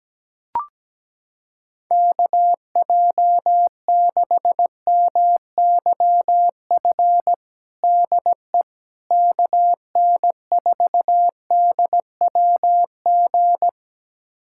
Morse code, heard as a keyed tone, about 17 wpm, 700 Hz. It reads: KJ6MYF DE KN4DWG
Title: USA Callsign DE Callsign 17wpm